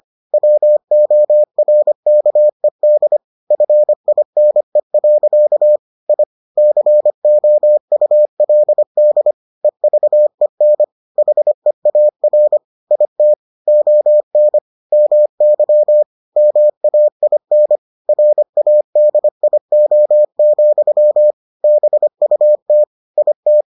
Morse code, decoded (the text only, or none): WORKED FINE. I COULD EVEN HEAR IT ON MY MAIN RADIO, BUT IT